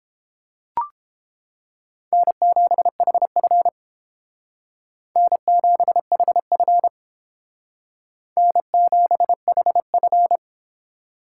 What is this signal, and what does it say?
Morse code, heard as a keyed tone, about 30 wpm, 700 Hz. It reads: N7HF N7HF N7HF